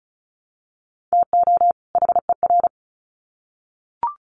Morse code, heard as a keyed tone, about 34 wpm, 700 Hz. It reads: TO HER
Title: Sets of 2 Words 35wpm